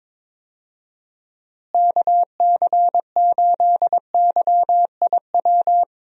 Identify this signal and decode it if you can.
Morse code, heard as a keyed tone, about 22 wpm, 700 Hz. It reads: KC8YIW
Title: USA Call Signs - 6 Characters - Repeated 3x - 22wpm